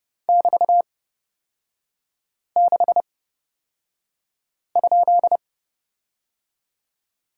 Morse code, heard as a keyed tone, about 31 wpm, spaced as for 5 wpm, 700 Hz. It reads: <BT>6?
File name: Lesson 36e - BT - Three Characters 30wpm